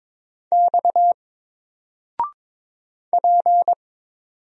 Morse code, heard as a keyed tone, about 22 wpm, 700 Hz. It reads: X P
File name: Single-Letter-Number-22wpm